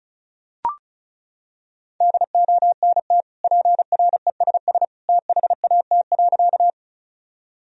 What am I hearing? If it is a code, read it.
Morse code, heard as a keyed tone, about 36 wpm, 700 Hz. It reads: DONT PRESS THAT.
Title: Sentences from Top 1000 Words - Review Part 1 - 35wpm